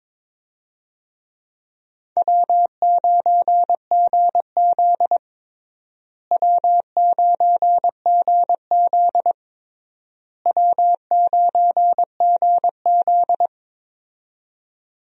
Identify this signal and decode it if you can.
Morse code, heard as a keyed tone, about 22 wpm, 700 Hz. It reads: W9GZ W9GZ W9GZ
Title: USA Call Signs - 4 Characters - Repeated 3x - 22wpm